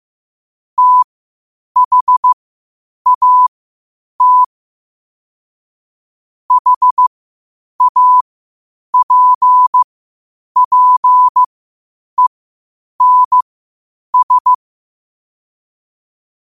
Morse code, heard as a keyed tone, about 15 wpm, spaced as for 8 wpm, 1000 Hz. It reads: THAT HAPPENS
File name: qotdfiles / 8wpm